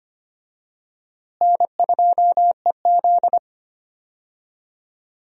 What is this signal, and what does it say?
Morse code, heard as a keyed tone, about 25 wpm, 700 Hz. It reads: N2EZ
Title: USA Call Signs - 4 Characters - Repeated 3x - 25wpm